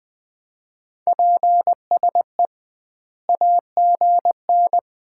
Morse code, heard as a keyed tone, about 20 wpm, 700 Hz. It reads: PSE AGN
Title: Common QSO Phrases - 2x Word-Spacing - 20wpm